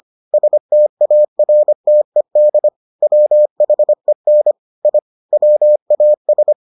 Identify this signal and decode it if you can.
Morse code, heard as a keyed tone, about 25 wpm, 600 Hz. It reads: STARTED WHEN I WAS